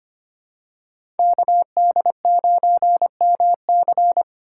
Morse code, heard as a keyed tone, about 25 wpm, 700 Hz. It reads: KD9MC
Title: USA Call Signs - 5 Characters - 2x Character Spacing - 25wpm